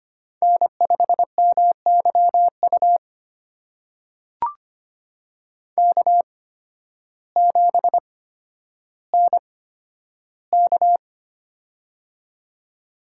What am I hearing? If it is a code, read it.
Morse code, heard as a keyed tone, about 25 wpm, spaced as for 12 wpm, 700 Hz. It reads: N5MYU K7NK
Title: USA Call Signs - 8x Character Spacing - 25wpm